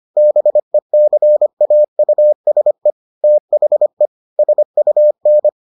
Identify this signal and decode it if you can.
Morse code, heard as a keyed tone, about 25 wpm, 600 Hz. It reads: BECAUSE THE SUN